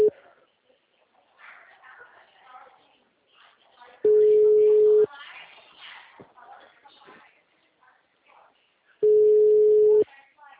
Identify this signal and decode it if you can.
Ringback tone (Europe, 425 Hz), with speech.